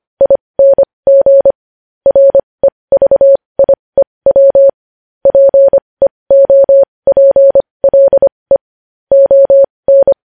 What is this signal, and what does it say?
Morse code, heard as a keyed tone, about 25 wpm, spaced as for 20 wpm, 550 Hz. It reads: ING REVIEW PEOPLE ON
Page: news-headlines-in-morse-code-at-20-wpm